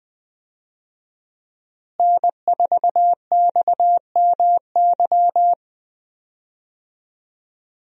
Morse code, heard as a keyed tone, about 20 wpm, 700 Hz. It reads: N4XMY
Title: USA Call Signs - 5 Characters - Repeated 3x - 20wpm